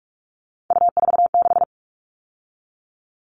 Morse code, heard as a keyed tone, about 44 wpm, 700 Hz. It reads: U46
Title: All Cut Numbers - 3-Digit 45wpm